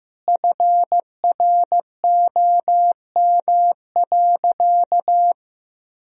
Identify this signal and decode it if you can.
Morse code, heard as a keyed tone, about 16 wpm, 700 Hz. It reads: FROM.
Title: Sentences from Top 1000 Words - Review Part 4 - 15wpm